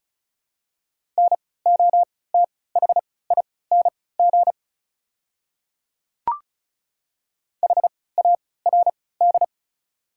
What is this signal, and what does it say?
Morse code, heard as a keyed tone, about 34 wpm, spaced as for 20 wpm, 690 Hz. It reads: NOTHING HARD
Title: 3x Character Spacing - Top 500 Words - 35wpm